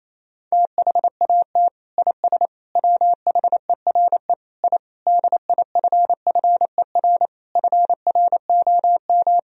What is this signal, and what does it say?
Morse code, heard as a keyed tone, about 28 wpm, 700 Hz. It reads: THAT IS WHERE I DIFFER FROM
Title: Sentences from Top 1000 Words - Review Part 1 - 28wpm